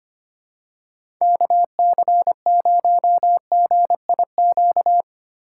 Morse code, heard as a keyed tone, about 25 wpm, 700 Hz. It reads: KC0GIQ
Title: USA Call Signs - 6 Characters - 4x Character Spacing - 25wpm